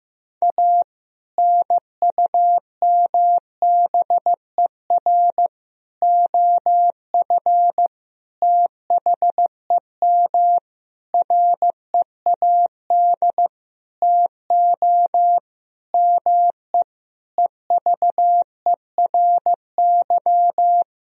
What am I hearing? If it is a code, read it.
Morse code, heard as a keyed tone, about 15 wpm, 700 Hz. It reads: A NUMBER OF THEM READ TO ME EVERY
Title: Sentences from Top 1000 Words - Review Part 1 - 15wpm